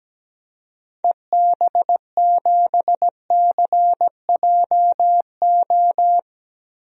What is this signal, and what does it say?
Morse code, heard as a keyed tone, about 17 wpm, 700 Hz. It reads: EB7CJO
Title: DXCC-100 Call Signs - 6 Characters - Repeated 3x - 17wpm